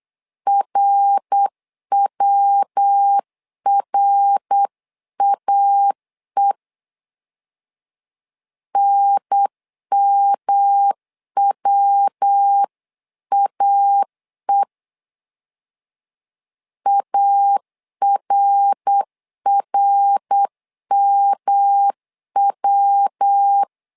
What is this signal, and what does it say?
Morse code, heard as a keyed tone, about 8 wpm, 800 Hz. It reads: RWRAE NMWAE ARRMW
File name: LEZIONI / 3 MWRO / 3 MWR 8wpm 2444